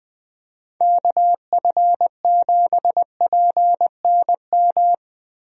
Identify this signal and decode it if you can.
Morse code, heard as a keyed tone, about 20 wpm, 700 Hz. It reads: KF7PNM
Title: USA Call Signs - 6 Characters - Repeated 2x - 20wpm